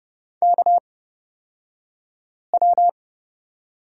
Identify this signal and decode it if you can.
Morse code, heard as a keyed tone, about 31 wpm, spaced as for 5 wpm, 700 Hz. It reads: KW
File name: Lesson 38d - Letter q - Two Characters 30wpm